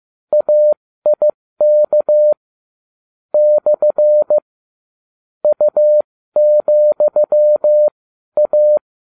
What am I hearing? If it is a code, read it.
Morse code, heard as a keyed tone, about 15 wpm, spaced as for 12 wpm, 600 Hz. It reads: AIK / U,A